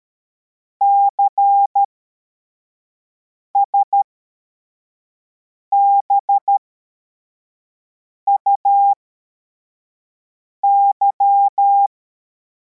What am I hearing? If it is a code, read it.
Morse code, heard as a keyed tone, about 13 wpm, spaced as for 4 wpm, 800 Hz. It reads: CSBUY
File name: formation / cw / Text64